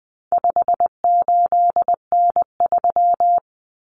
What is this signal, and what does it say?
Morse code, heard as a keyed tone, about 20 wpm, 700 Hz. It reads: 58N3
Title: Common Cut Numbers - 4-Digit 20wpm